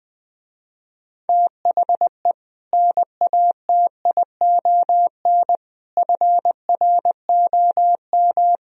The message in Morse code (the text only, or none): THE NATION FROM